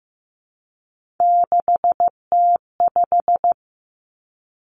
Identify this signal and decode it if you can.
Morse code, heard as a keyed tone, about 15 wpm, 700 Hz. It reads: 6T5
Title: Three Challenging Letter-Number Combinations 15wpm